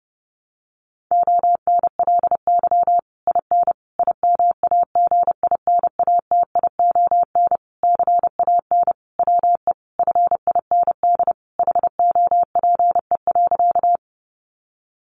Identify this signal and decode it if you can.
Morse code, heard as a keyed tone, about 30 wpm, 700 Hz. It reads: ONLY IN IMAGINATION CAN WE FIND HOPE.
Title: Sentences from Top 1300 Words 30wpm